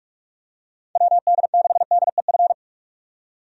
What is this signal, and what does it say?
Morse code, heard as a keyed tone, about 44 wpm, 700 Hz. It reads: WD6DEF
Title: USA Call Signs - 6 Characters - 45wpm